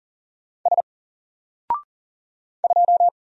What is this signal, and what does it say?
Morse code, heard as a keyed tone, about 39 wpm, 700 Hz. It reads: S 2
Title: Single Letter-Number 40wpm